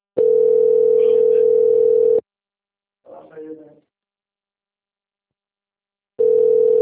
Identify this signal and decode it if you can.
Ringback tone (North America), 440 + 480 Hz.